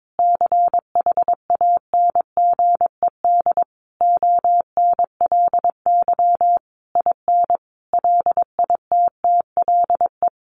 Morse code, heard as a keyed tone, about 22 wpm, 700 Hz. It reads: CHANGED ONLY IN LITTLE
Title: A Sound Of Thunder 22wpm